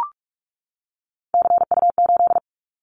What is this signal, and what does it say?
Morse code, heard as a keyed tone, about 44 wpm, 700 Hz. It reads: CU8
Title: Callsign Prefixes for Top 100 DXCC - 45wpm